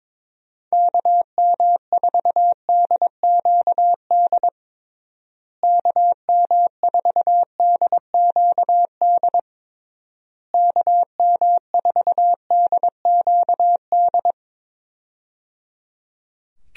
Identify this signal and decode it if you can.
Morse code, heard as a keyed tone, about 22 wpm, 700 Hz. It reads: KM4DQD KM4DQD KM4DQD
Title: USA Call Signs - 6 Characters - Repeated 3x - 22wpm